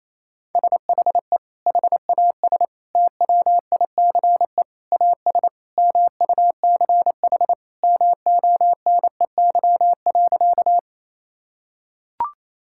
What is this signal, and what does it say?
Morse code, heard as a keyed tone, about 27 wpm, 700 Hz. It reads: SHE HAS TWICE AS MUCH MONEY.
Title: Sentences from Top 1600 Words 28wpm